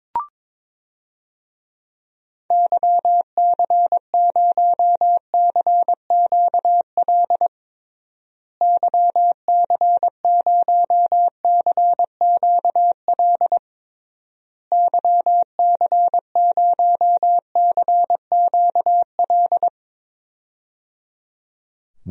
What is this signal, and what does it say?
Morse code, heard as a keyed tone, about 22 wpm, 700 Hz. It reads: YC0CQL YC0CQL YC0CQL
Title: DXCC-100 Call Signs - 6 Characters - Repeated 3x - 22wpm